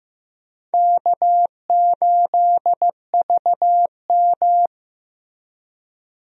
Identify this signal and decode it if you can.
Morse code, heard as a keyed tone, about 15 wpm, 700 Hz. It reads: K8VM
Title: USA Call Signs - 4 Characters - 15wpm